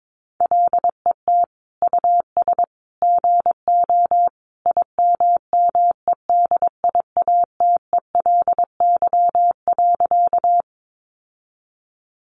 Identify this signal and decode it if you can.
Morse code, heard as a keyed tone, about 22 wpm, 700 Hz. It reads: LET US GO IMMEDIATELY.
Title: Sentences from Top 1100 Words 22wpm